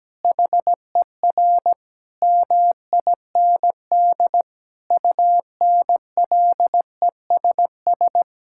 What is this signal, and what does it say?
Morse code, heard as a keyed tone, about 17 wpm, 700 Hz. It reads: HER MIND UNLESS
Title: Sentences from Top 1000 Words - Review Part 6 - 17wpm